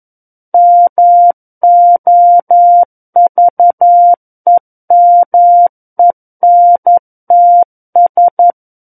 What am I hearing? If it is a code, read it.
Morse code, heard as a keyed tone, about 11 wpm, 700 Hz. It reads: MOVEMENTS
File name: Book One / 02-Chapter 2 - 11 WPM